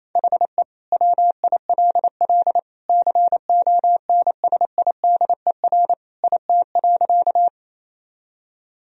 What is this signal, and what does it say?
Morse code, heard as a keyed tone, about 28 wpm, 700 Hz. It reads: HE WILL CONSIDER IT.
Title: Sentences from Top 1000 Words - Review Part 4 - 28wpm